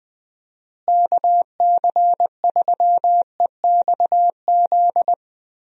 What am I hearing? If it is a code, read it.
Morse code, heard as a keyed tone, about 21 wpm, 700 Hz. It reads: KC3EXZ
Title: USA Call Signs - 6 Characters - Repeated 3x - 20wpm